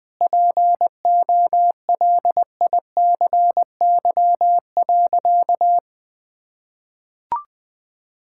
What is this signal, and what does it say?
Morse code, heard as a keyed tone, about 20 wpm, 700 Hz. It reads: POLICY.
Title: Sentences from Top 1000 Words - Review Part 5 - 20wpm